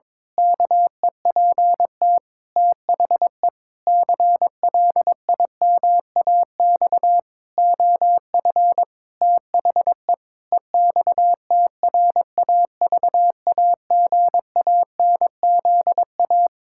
Morse code, heard as a keyed tone, about 21 wpm, 700 Hz. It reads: KEPT THE CLIMAX OF THE EXTRAVAGANZA
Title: Pangram Challenge v2 - 22wpm